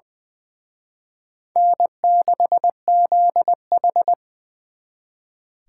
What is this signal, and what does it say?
Morse code, heard as a keyed tone, about 24 wpm, 700 Hz. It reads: N6ZH
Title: Speed Racing - USA 1x2 Call Signs 20wpm